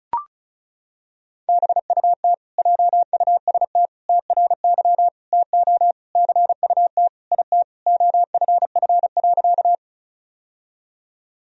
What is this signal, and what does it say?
Morse code, heard as a keyed tone, about 36 wpm, 700 Hz. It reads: BUT JUST TRY TO CUT IT OFF.
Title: Sentences from Top 1000 Words - Review Part 1 - 35wpm